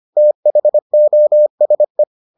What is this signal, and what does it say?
Morse code, heard as a keyed tone, about 25 wpm, 600 Hz. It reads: THOSE